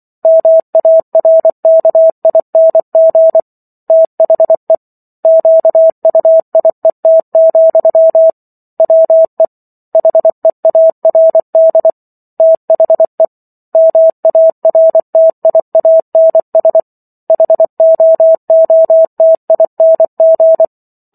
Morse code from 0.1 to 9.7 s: MARKING THE QUIET, WE

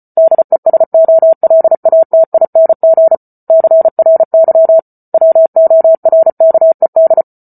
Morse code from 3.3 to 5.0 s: CRY

